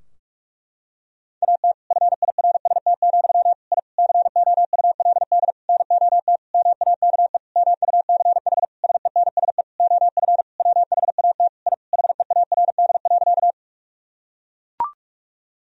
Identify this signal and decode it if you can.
Morse code, heard as a keyed tone, about 41 wpm, 700 Hz. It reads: AT FIRST, I COULD NOT MAKE MUCH SENSE OF WHAT I HEARD.
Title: Sentences from Top 1000 Words - Review Part 4 - 45wpm